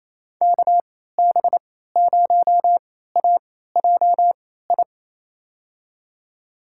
Morse code, heard as a keyed tone, about 28 wpm, spaced as for 16 wpm, 700 Hz. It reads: KB0AJI
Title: USA Call Signs - 6 Characters - 3x Character Spacing - 28wpm